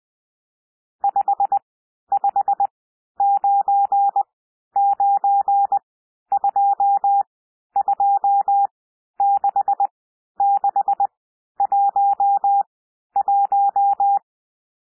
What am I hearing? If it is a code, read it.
Morse code, heard as a keyed tone, about 20 wpm, spaced as for 11 wpm, 800 Hz. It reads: 5599226611
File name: SOUND NUMBERS 20 wpm